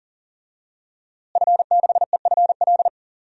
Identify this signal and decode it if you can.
Morse code, heard as a keyed tone, about 40 wpm, 700 Hz. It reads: F6EFL
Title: DXCC-100 Call Signs - 5 Characters - Repeated 2x - 40wpm